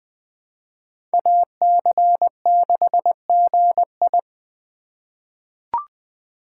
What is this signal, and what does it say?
Morse code, heard as a keyed tone, about 20 wpm, 700 Hz. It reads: AC6GI
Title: USA Call Signs - 5 Characters - 3x Character Spacing - 20wpm